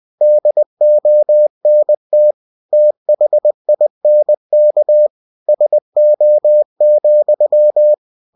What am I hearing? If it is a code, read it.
Morse code, heard as a keyed tone, about 20 wpm, 600 Hz. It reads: DONT THINK SO,
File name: cw-books / winnie-the-pooh / winnie-20wpm-page-0014